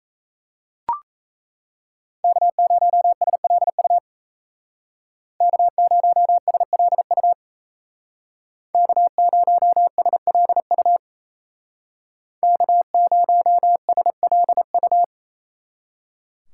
Morse code, heard as a keyed tone, about 36 wpm, 700 Hz. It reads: K0SLU K0SLU K0SLU K0SLU